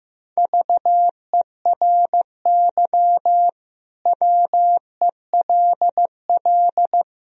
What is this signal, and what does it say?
Morse code, heard as a keyed tone, about 15 wpm, 700 Hz. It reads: VERY WELL
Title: Sentences from Top 1000 Words - Review Part 3 - 15wpm